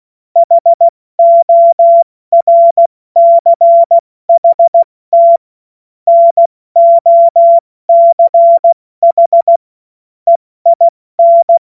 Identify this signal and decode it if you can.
Morse code, heard as a keyed tone, about 16 wpm, 680 Hz. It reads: HORCHT NOCH EIN
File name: Morsekurs / Witze / Witz 21. Juni 2022 70 F80